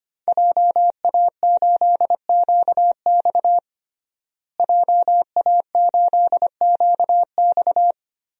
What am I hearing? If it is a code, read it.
Morse code, heard as a keyed tone, about 25 wpm, 700 Hz. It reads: JA8QX JA8QX
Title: DXCC-100 Call Signs - 5 Characters - Repeated 2x - 25wpm